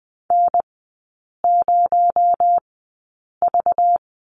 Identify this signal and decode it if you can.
Morse code, heard as a keyed tone, about 20 wpm, spaced as for 8 wpm, 700 Hz. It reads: N0V
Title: 218 Lesson 35e - Number 0 - Three Characters 20wpm